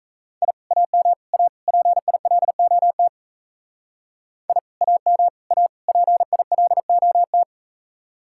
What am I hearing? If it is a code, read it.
Morse code, heard as a keyed tone, about 45 wpm, 700 Hz. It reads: I AM A PILOT I AM A PILOT